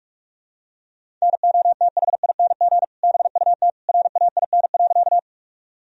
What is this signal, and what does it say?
Morse code, heard as a keyed tone, about 46 wpm, 700 Hz. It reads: NOTHING BUT RAIN.
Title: Sentences from Top 1000 Words - Review Part 1 - 45wpm